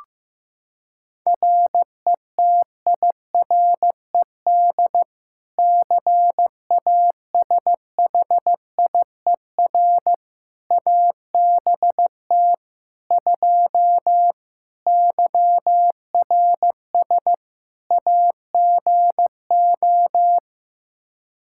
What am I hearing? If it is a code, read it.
Morse code, heard as a keyed tone, about 15 wpm, 700 Hz. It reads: RETIRED CASHIER ABT 2 YRS AGO
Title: K1USN SST QSO Practice 15wpm